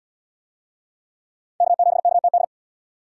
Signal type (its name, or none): Morse code